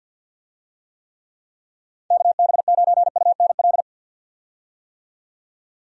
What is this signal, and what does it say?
Morse code, heard as a keyed tone, about 51 wpm, 700 Hz. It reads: KB9UNL
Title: USA Call Signs - 6 Characters - 50wpm